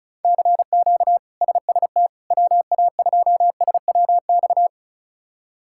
Morse code, heard as a keyed tone, about 36 wpm, 700 Hz. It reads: CQ SST WA2SWX
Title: K1USN SST QSO Practice - 35wpm